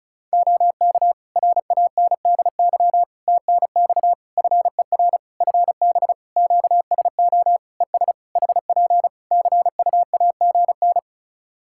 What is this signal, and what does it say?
Morse code, heard as a keyed tone, about 36 wpm, 700 Hz. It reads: OK RANDY TNX FER FB QSO ES HP CUAGN
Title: Practice QSO - Exchange 4 - Ending - 35wpm